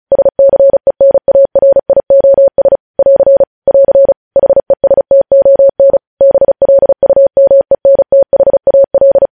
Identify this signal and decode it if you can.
Morse code, heard as a keyed tone, about 36 wpm, 550 Hz. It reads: SCENARIOS <AR> <AR> HESTON BLUMENTHAL
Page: news headlines in morse code at 35 wpm